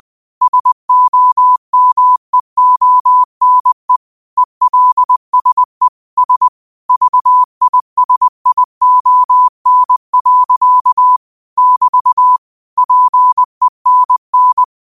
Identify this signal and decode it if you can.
Morse code, heard as a keyed tone, about 20 wpm, 1000 Hz. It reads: SOMEONE ELSE S VISION. <BT> PENN
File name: qotdfiles / 20wpm